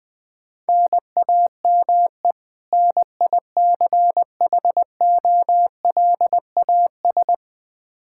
Morse code, heard as a keyed tone, about 20 wpm, 700 Hz. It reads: NAME NICHOLAS
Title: K1USN SST QSO Practice 20wpm